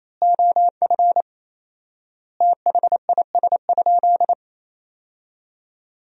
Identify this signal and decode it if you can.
Morse code, heard as a keyed tone, about 27 wpm, 700 Hz. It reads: OF THIS?
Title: 4x Word-Spacing - Sentences from Top 200 Words - 28wpm